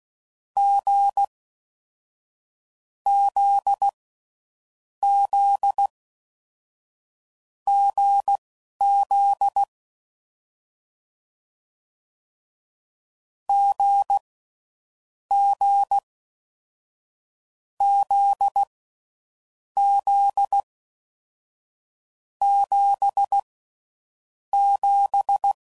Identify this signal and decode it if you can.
Morse code, heard as a keyed tone, about 16 wpm, spaced as for 11 wpm, 800 Hz. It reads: G Z Z GZ G G Z Z 7 7
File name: Mixed / Short Runs / awjndb6gz7